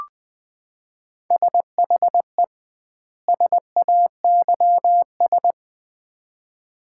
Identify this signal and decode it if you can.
Morse code, heard as a keyed tone, about 20 wpm, 700 Hz. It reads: SHE SAYS
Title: 2x Word-Spacing - Sets of 2 Words - 20wpm